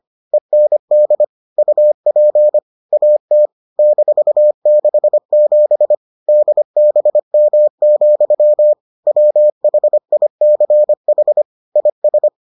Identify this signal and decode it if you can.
Morse code, heard as a keyed tone, about 25 wpm, 600 Hz. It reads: END UP AT -67 DBM, WHICH IS